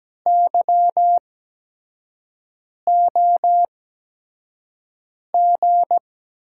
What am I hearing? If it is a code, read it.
Morse code, heard as a keyed tone, about 17 wpm, spaced as for 5 wpm, 700 Hz. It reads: YOG
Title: USA Call Signs - 6 Characters - 8x Character Spacing - 17wpm